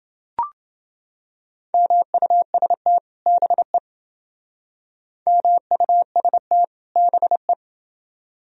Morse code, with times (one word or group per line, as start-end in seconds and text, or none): 1.7-3.0 MUST
3.3-3.8 BE
5.3-6.6 MUST
7.0-7.5 BE